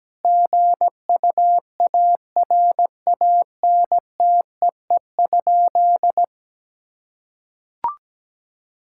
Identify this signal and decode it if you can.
Morse code, heard as a keyed tone, about 17 wpm, 700 Hz. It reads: GUARANTEE?
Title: Sentences from Top 1600 Words 17wpm